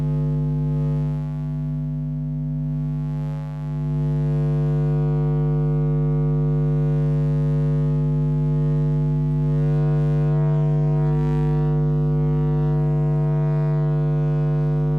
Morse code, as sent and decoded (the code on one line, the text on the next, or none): none